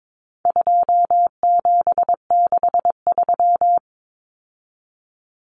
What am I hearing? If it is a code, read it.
Morse code, heard as a keyed tone, about 22 wpm, 700 Hz. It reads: 2763
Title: Common Cut Numbers - 4-Digit 22wpm